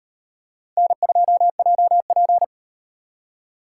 Morse code, heard as a keyed tone, about 33 wpm, 700 Hz. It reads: N2JP